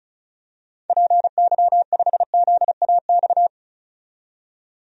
Morse code, heard as a keyed tone, about 34 wpm, 700 Hz. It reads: PY5ZAX
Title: DXCC-100 Call Signs - 6 Characters - 35wpm